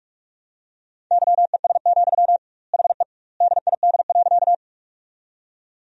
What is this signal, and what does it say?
Morse code, heard as a keyed tone, about 46 wpm, 700 Hz. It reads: YES, HE DID.
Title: 2x Word-Spacing - Sentences from Top 500 Words - 45wpm